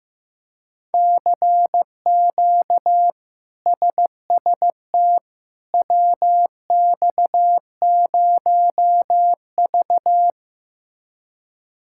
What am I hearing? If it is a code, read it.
Morse code, heard as a keyed tone, about 15 wpm, 700 Hz. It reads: CQ SST WX0V
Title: K1USN SST QSO Practice - 15wpm